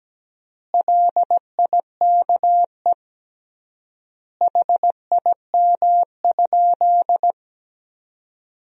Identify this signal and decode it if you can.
Morse code, heard as a keyed tone, about 17 wpm, 700 Hz. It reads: LIKE HIM?
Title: 3x Word-Spacing - Sentences from Top 100 Words - 17wpm